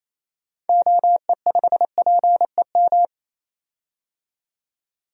Morse code, heard as a keyed tone, about 28 wpm, 700 Hz. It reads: OE5PEM
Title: DXCC-100 Call Signs - 6 Characters - 28wpm